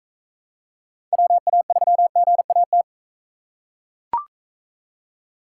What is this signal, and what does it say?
Morse code, heard as a keyed tone, about 39 wpm, 700 Hz. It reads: WA3GAT